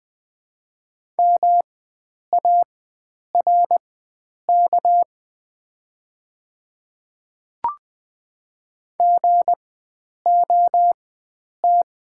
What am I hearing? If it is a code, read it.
Morse code, heard as a keyed tone, about 20 wpm, spaced as for 9 wpm, 700 Hz. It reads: MARK GOT
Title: 4x Character Spacing - Top 300 Words - 20wpm